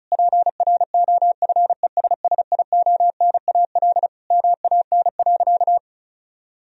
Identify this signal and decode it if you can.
Morse code, heard as a keyed tone, about 36 wpm, 700 Hz. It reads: PROFESSIONAL MAN.
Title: Sentences from Top 1000 Words - Review Part 4 - 35wpm